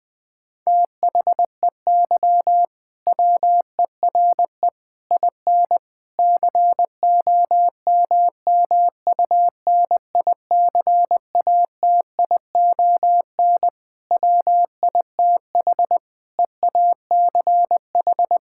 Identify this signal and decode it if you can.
Morse code, heard as a keyed tone, about 21 wpm, 700 Hz. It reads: THEY WERE IN COMMUNICATION WITH EACH
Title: Sentences from Top 1000 Words - Review Part 6 - 20wpm